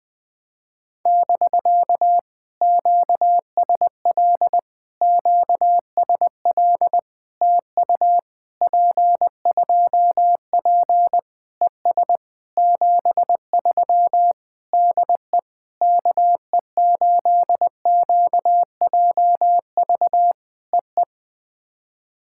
Morse code, heard as a keyed tone, about 20 wpm, 700 Hz. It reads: <BK> QSL QSL TU P2P ES 73 DE KE8QJV EE
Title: POTA QSO Practice - 20wpm